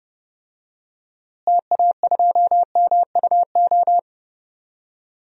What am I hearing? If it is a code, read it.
Morse code, heard as a keyed tone, about 29 wpm, 700 Hz. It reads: TA2MUO
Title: DXCC-100 Call Signs - 6 Characters - Repeated 3x - 30wpm